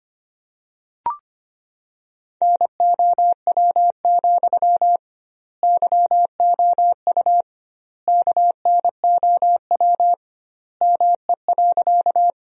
Morse code, heard as a keyed tone, about 25 wpm, 700 Hz. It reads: NOW, YOU KNOW ME.
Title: 2x Word-Spacing - Sentences from Top 200 Words - 25wpm